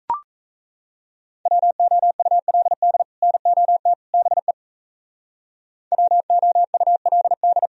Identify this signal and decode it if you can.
Morse code, heard as a keyed tone, about 34 wpm, 700 Hz. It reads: WOULD NOT BE WOULD
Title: Speed-Racing - Sets of 3 Words 28wpm